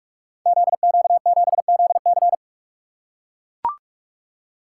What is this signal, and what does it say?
Morse code, heard as a keyed tone, about 44 wpm, 700 Hz. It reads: ZQ7ZC
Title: Code Groups - 5 Letter:Numbers - 45wpm